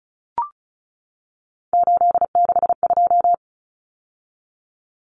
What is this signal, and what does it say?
Morse code, heard as a keyed tone, about 34 wpm, 700 Hz. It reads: 862
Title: Numbers - 3-Digit 35wpm